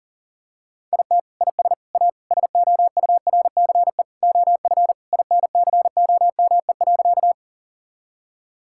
Morse code, heard as a keyed tone, about 40 wpm, 700 Hz. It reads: IT IS A SOURCE OF INCOME.
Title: Sentences from Top 1000 Words - Review Part 5 - 40wpm